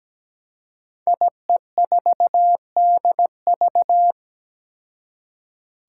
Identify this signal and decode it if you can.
Morse code, heard as a keyed tone, about 17 wpm, 700 Hz. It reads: IE4DV
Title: Five Challenging Letter-Number Combinations 17wpm